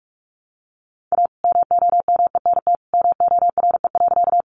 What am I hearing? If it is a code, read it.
Morse code, heard as a keyed tone, about 44 wpm, 700 Hz. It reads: A MOMENT MORE.
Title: Sentences from Top 1000 Words 45wpm